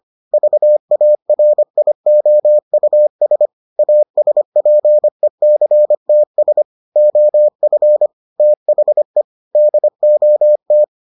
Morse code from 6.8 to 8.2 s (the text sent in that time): OF